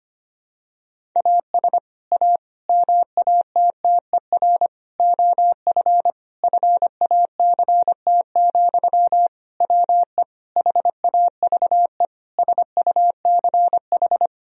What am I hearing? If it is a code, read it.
Morse code, heard as a keyed tone, about 25 wpm, 700 Hz. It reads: AS A MATTER OF FACT, WE HAVE SUCH
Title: Sentences from Top 1000 Words - Review Part 3 - 25wpm